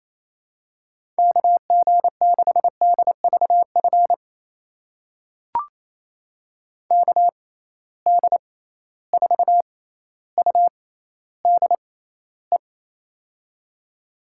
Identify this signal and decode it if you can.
Morse code, heard as a keyed tone, about 28 wpm, spaced as for 14 wpm, 700 Hz. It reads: KG6DVF KD4UDE
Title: USA Call Signs - 6 Characters - 6x Character Spacing - 28wpm